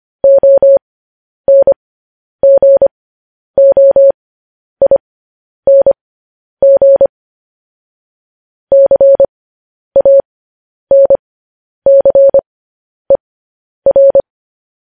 Morse code, heard as a keyed tone, about 25 wpm, spaced as for 10 wpm, 550 Hz. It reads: ONGOING CANCER
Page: news-headlines-in-morse-code-at-10-wpm